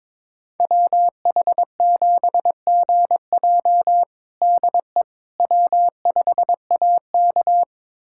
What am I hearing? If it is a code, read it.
Morse code, heard as a keyed tone, about 22 wpm, 700 Hz. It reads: WH7GJ DE W5AK
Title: USA Callsign DE Callsign 22wpm